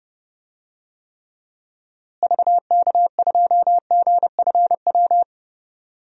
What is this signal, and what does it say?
Morse code, heard as a keyed tone, about 32 wpm, 700 Hz. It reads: VK2GFW
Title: DXCC-100 Call Signs - 6 Characters - Repeated 2x - 30wpm